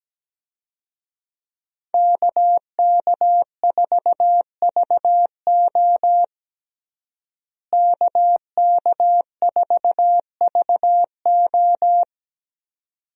Morse code, heard as a keyed tone, about 17 wpm, 700 Hz. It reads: KK4VO KK4VO
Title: USA Call Signs - 5 Characters - Repeated 2x - 17wpm